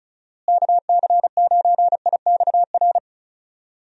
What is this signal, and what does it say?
Morse code, heard as a keyed tone, about 34 wpm, 700 Hz. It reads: KC9IXR
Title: USA Call Signs - 6 Characters - Repeated 3x - 35wpm